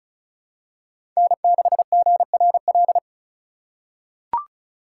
Morse code, heard as a keyed tone, about 36 wpm, 700 Hz. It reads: N6GRL
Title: USA Call Signs - 4x Character Spacing - 35wpm